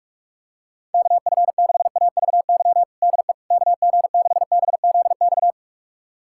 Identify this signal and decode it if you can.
Morse code, heard as a keyed tone, about 46 wpm, 700 Hz. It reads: KF6AVY DE KG6BZX
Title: USA Callsign DE Callsign - 6 Characters - 45wpm